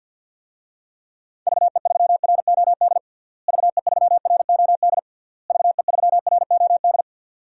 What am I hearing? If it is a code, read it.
Morse code, heard as a keyed tone, about 47 wpm, 700 Hz. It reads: VE3ROD VE3ROD VE3ROD
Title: DXCC-100 Call Signs - 6 Characters - Repeated 3x - 50wpm